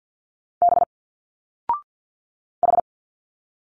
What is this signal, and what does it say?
Morse code, heard as a keyed tone, about 50 wpm, 700 Hz. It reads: B H
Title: Single Letters 50wpm